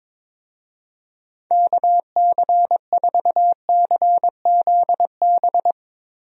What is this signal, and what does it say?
Morse code, heard as a keyed tone, about 21 wpm, 700 Hz. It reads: KC4CZB
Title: USA Call Signs - 6 Characters - Repeated 2x - 22wpm